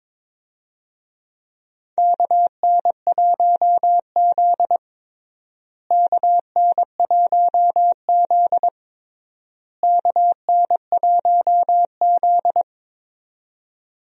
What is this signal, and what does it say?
Morse code, heard as a keyed tone, about 22 wpm, 700 Hz. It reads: KN1Z KN1Z KN1Z
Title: USA Call Signs - 4 Characters - Repeated 3x - 22wpm